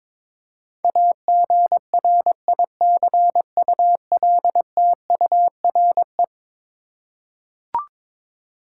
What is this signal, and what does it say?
Morse code, heard as a keyed tone, about 22 wpm, 700 Hz. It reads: AGRICULTURE
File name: practice / 22wpm / Eleven Letter Words - 22wpm